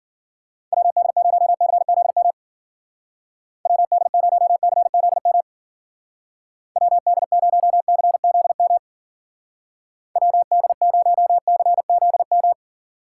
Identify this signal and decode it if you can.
Morse code, heard as a keyed tone, about 49 wpm, 700 Hz. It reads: WD0CZM WD0CZM WD0CZM WD0CZM
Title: Speed Racing - USA 2x3 Call Signs 40wpm